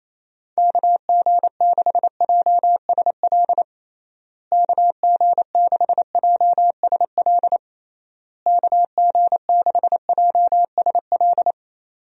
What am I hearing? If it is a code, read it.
Morse code, heard as a keyed tone, about 27 wpm, 700 Hz. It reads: KG6JSL KG6JSL KG6JSL
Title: USA Call Signs - 6 Characters - Repeated 3x - 28wpm